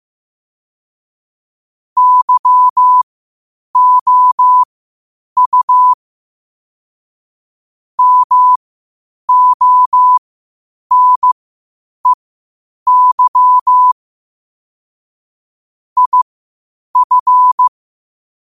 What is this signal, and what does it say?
Morse code, heard as a keyed tone, about 15 wpm, spaced as for 8 wpm, 1000 Hz. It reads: YOU MONEY IF
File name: qotdfiles / 8wpm